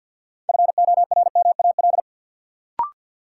Morse code, heard as a keyed tone, about 46 wpm, 700 Hz. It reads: FORMAL
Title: Sentences from Top 1300 Words - New Word Drill - 50wpm